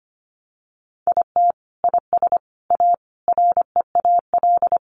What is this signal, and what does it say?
Morse code, heard as a keyed tone, about 25 wpm, 700 Hz. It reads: IT IS A REAL
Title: Sentences from Top 500 Words 25wpm